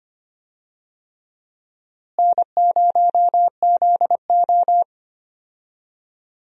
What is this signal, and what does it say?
Morse code, heard as a keyed tone, about 25 wpm, 700 Hz. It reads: N0ZO
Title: USA Call Signs - 4 Characters - Repeated 3x - 25wpm